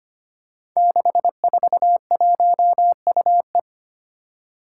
Morse code, heard as a keyed tone, about 25 wpm, 700 Hz. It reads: 641UE
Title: Five Challenging Letter-Number Combinations 25wpm